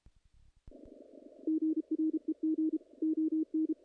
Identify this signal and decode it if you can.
Morse code, heard as a keyed tone, about 32 wpm, 320 Hz. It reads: GREG ON
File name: contests / naqp-cw-k5tr-op-k5pi- -1800